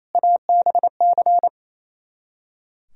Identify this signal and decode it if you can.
Morse code, heard as a keyed tone, about 38 wpm, spaced as for 33 wpm, 700 Hz. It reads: ABC
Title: Speed-Racing - Sets of 5 Words 28wpm